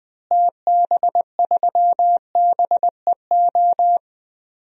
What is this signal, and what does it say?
Morse code, heard as a keyed tone, about 20 wpm, 700 Hz. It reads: TB3BEO
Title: DXCC-100 Call Signs - 6 Characters - Repeated 2x - 20wpm